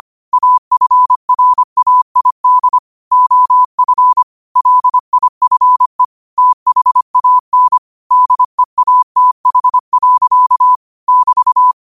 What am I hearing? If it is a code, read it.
Morse code, heard as a keyed tone, about 25 wpm, 1000 Hz. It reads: AFRAID OF LIFE THAN DEATH. <BT>
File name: qotdfiles / 25wpm